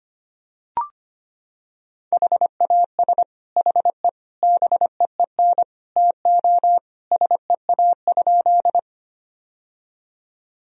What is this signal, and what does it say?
Morse code, heard as a keyed tone, about 25 wpm, 700 Hz. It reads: HAS HE BEEN TO SEA?
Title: Sentences from Top 1000 Words - Review Part 1 - 25wpm